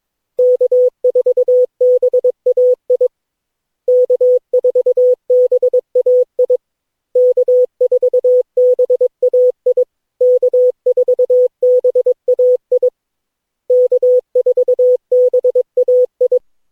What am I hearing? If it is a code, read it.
Morse code, heard as a keyed tone, about 22 wpm, 500 Hz. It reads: K4BAI K4BAI K4BAI K4BAI K4BAI
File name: cw / Session9 cw training